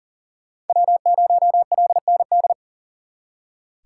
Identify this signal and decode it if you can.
Morse code, heard as a keyed tone, about 52 wpm, 700 Hz. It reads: W0LND